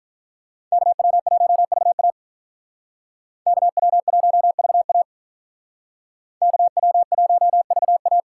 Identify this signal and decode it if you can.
Morse code, heard as a keyed tone, about 44 wpm, 700 Hz. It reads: KW1VA KW1VA KW1VA
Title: Speed Racing - USA 2x2 Call Signs 35wpm